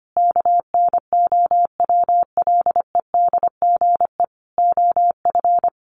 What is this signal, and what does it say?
Morse code, heard as a keyed tone, about 25 wpm, 700 Hz. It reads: KNOWLEDGE OF
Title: Sentences from Top 600 Words 25wpm